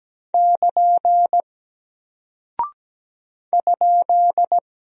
Morse code, heard as a keyed tone, about 17 wpm, 700 Hz. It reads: <KN> ?
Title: Single Character (Letter:Number:Punctuation:Prosign) Alt 17wpm